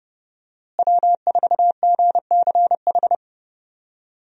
Morse code, heard as a keyed tone, about 32 wpm, 700 Hz. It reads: W4GCH
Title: USA Call Signs - 5 Characters - Repeated 3x - 30wpm